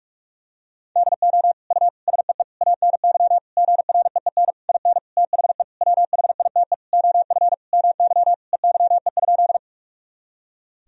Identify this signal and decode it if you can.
Morse code, heard as a keyed tone, about 41 wpm, 700 Hz. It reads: DO U SEE ANY GREEN IN THE WHITE OF MY EYE?
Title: Sentences from Top 500 Words and Abbreviations 45wpm